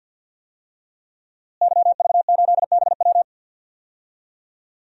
Morse code, heard as a keyed tone, about 52 wpm, 700 Hz. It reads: YV8BW
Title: DXCC-100 Call Signs - 5 Characters - Repeated 2x - 50wpm